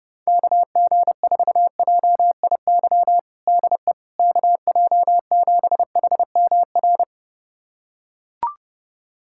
Morse code, heard as a keyed tone, about 30 wpm, 700 Hz. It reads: KG4JIY DE KJ7HMR
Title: USA Callsign DE Callsign 30wpm